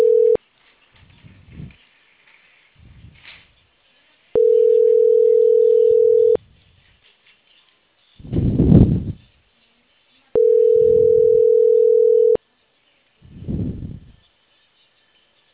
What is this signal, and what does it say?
Ringback tone (North America), 440 + 480 Hz.